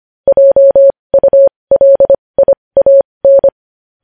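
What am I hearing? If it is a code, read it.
Morse code, heard as a keyed tone, about 25 wpm, spaced as for 20 wpm, 550 Hz. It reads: JULIAN
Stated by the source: Morse code news headlines sent at 20 words per minute.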